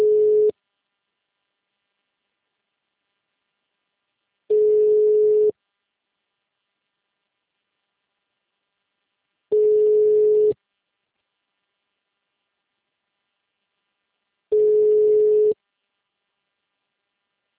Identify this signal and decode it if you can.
Ringback tone (Europe, 425 Hz).